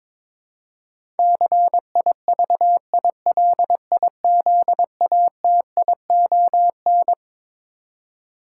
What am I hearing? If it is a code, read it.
Morse code, heard as a keyed tone, about 22 wpm, 700 Hz. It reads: CIVILIZATION
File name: Twelve Letter Words - 22wpm